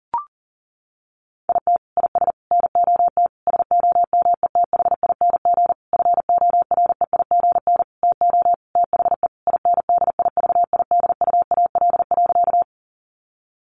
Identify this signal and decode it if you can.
Morse code, heard as a keyed tone, about 40 wpm, 700 Hz. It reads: IT IS NOT SOMETHING FOREIGN TO THE INDIVIDUAL.
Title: Sentences from Top 1400 Words 40wpm